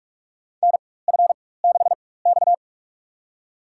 Morse code, heard as a keyed tone, about 46 wpm, spaced as for 21 wpm, 700 Hz. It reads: NF6X
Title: USA Call Signs - 4x Character Spacing - 45wpm